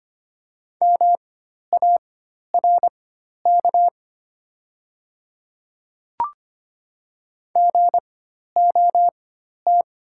Morse code, heard as a keyed tone, about 25 wpm, spaced as for 12 wpm, 700 Hz. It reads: MARK GOT
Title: 4x Character Spacing - Top 300 Words - 25wpm